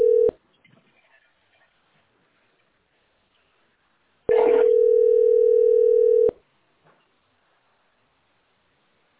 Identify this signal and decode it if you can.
Ringback tone (North America), 440 + 480 Hz.